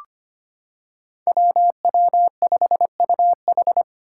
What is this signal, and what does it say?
Morse code, heard as a keyed tone, about 25 wpm, 700 Hz. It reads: WW5UH
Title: Five Challenging Letter-Number Combinations 25wpm